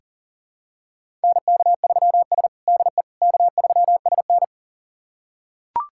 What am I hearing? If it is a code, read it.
Morse code, heard as a keyed tone, about 37 wpm, 700 Hz. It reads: NK3S DE K3SN
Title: USA Callsign DE Callsign - 4 Characters - 40wpm